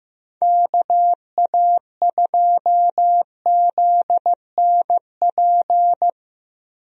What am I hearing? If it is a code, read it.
Morse code, heard as a keyed tone, about 15 wpm, 700 Hz. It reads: KA2ZNP
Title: USA Call Signs - 6 Characters - Repeated 2x - 15wpm